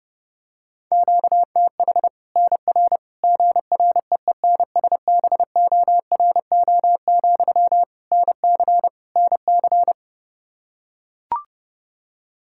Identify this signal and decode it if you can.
Morse code, heard as a keyed tone, about 30 wpm, 700 Hz. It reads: QTH NR GREENSBORO, NC NC
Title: Practice QSO - Exchange 1 - Beginning - 30wpm